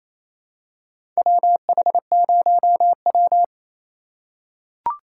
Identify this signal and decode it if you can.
Morse code, heard as a keyed tone, about 27 wpm, 700 Hz. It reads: WH0W
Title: USA Call Signs - 4 Characters - 4x Character Spacing - 28wpm